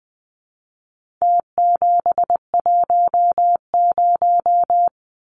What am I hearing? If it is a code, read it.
Morse code, heard as a keyed tone, about 20 wpm, 700 Hz. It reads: T710
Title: Numbers - Contest Style - 4-Digit 20wpm